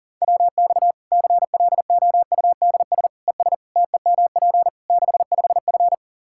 Morse code, heard as a keyed tone, about 40 wpm, 700 Hz. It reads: WX CLOUDS ES TEMP 65F
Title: Practice QSO - Exchange 2 - Info - 40wpm